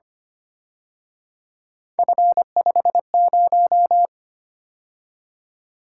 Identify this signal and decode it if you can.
Morse code, heard as a keyed tone, about 29 wpm, 700 Hz. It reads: F50
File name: Speed-Racing - Three Letter-Number Combinations - 25wpm